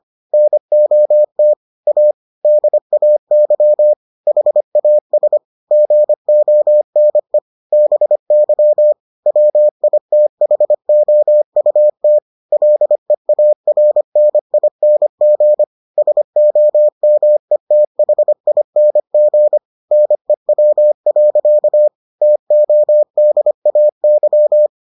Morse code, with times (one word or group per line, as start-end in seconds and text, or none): 0.3-1.5 NOT
1.9-2.1 A
2.5-3.9 DAY
4.3-5.4 HAS
5.7-7.4 GONE
7.7-8.9 BY
9.3-12.2 WITHOUT
12.5-15.6 LEARNING
16.0-19.6 SOMETHING
19.9-21.9 NEW.
22.2-24.7 TODAY